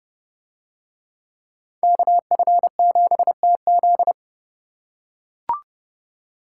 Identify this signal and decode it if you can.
Morse code, heard as a keyed tone, about 30 wpm, 700 Hz. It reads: KF7TZ